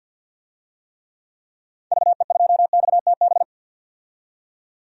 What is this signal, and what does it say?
Morse code, heard as a keyed tone, about 51 wpm, 700 Hz. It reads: VE2XTB
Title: DXCC-100 Call Signs - 6 Characters - 50wpm